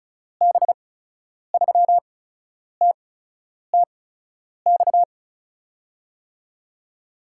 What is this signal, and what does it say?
Morse code, heard as a keyed tone, about 36 wpm, 700 Hz. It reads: B 3 T T X
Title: USA Call Signs - 6 Characters - 8x Character Spacing - 35wpm